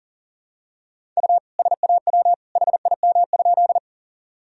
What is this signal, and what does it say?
Morse code, heard as a keyed tone, about 38 wpm, 700 Hz. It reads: U SAW HIM?
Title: Sentences from Top 300 Words and Abbreviations 40wpm